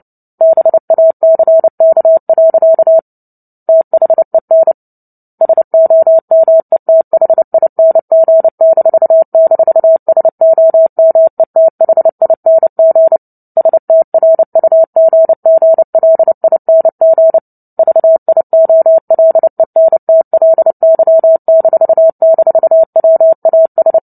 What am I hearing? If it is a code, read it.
Morse code, heard as a keyed tone, about 30 wpm, 650 Hz. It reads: BACK. THEN SOMETHING--SOMETHING STRUGGLING VIOLENTLY--WAS
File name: Book Two / 03-Chapter 3 - 29 WPM